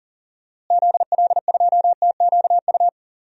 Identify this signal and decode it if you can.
Morse code, heard as a keyed tone, about 40 wpm, 700 Hz. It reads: ZL2TQU
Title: DXCC-100 Call Signs - 6 Characters - Repeated 2x - 40wpm